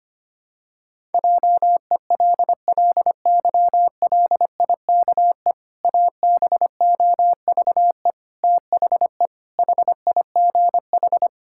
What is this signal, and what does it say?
Morse code, heard as a keyed tone, about 25 wpm, 700 Hz. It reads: JELLYLIKE ABOVE THE HIGH
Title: Pangram Challenge v2 - 25wpm